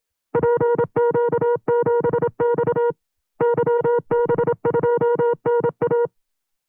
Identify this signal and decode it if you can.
Morse code, heard as a keyed tone, about 27 wpm, 480 Hz. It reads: PQ7X YB2NA
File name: CW-practice/25WPM